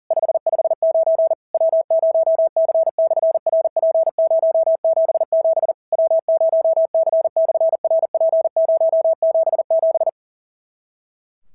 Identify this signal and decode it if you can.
Morse code, heard as a keyed tone, about 40 wpm, 650 Hz. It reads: 559 W0C/RP077 W0C/RP077
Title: SOTA QSO Practice 40wpm